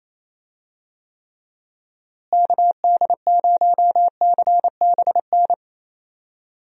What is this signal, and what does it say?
Morse code, heard as a keyed tone, about 28 wpm, 700 Hz. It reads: KD0CBN
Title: USA Call Signs - 6 Characters - Repeated 3x - 28wpm